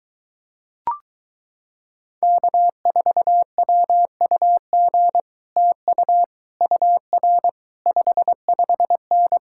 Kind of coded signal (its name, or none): Morse code